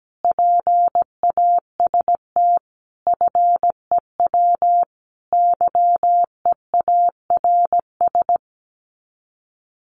Morse code, heard as a keyed tone, about 17 wpm, 700 Hz. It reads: PAST FEW YEARS
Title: Sets of 5 Words Encore 17wpm